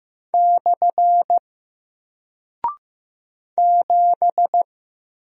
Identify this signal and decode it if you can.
Morse code, heard as a keyed tone, about 15 wpm, 700 Hz. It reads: / 7
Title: Single Character (Letter:Number:Punctuation:Prosign) Alt 15wpm